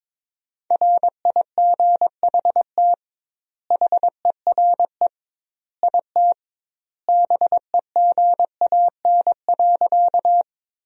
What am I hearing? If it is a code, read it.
Morse code, heard as a keyed tone, about 22 wpm, 700 Hz. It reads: RIGHT HERE IT BEGAN.
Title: 2x Word-Spacing - Sentences from Top 300 Words - 22wpm